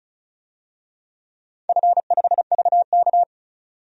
Morse code, heard as a keyed tone, about 36 wpm, 700 Hz. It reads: F5VK
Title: DXCC-100 Call Signs - 4 Characters - 35wpm